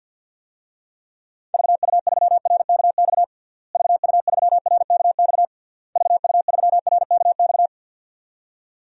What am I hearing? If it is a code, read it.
Morse code, heard as a keyed tone, about 47 wpm, 700 Hz. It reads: VU3RKX VU3RKX VU3RKX
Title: DXCC-100 Call Signs - 6 Characters - Repeated 3x - 50wpm